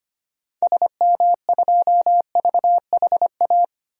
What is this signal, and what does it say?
Morse code, heard as a keyed tone, about 25 wpm, 700 Hz. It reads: SM2VHA
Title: DXCC-100 Call Signs - 6 Characters - Repeated 2x - 25wpm